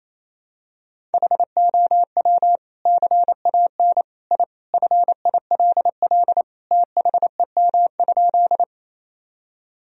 Morse code, heard as a keyed tone, about 28 wpm, 700 Hz. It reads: HOW CAN I FILL THEM?
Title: Sentences from Top 1000 Words - Review Part 3 - 28wpm